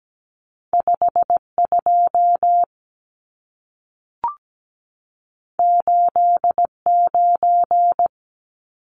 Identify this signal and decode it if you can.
Morse code, heard as a keyed tone, about 17 wpm, 700 Hz. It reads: 52 89
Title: Numbers - 2-Digit 17wpm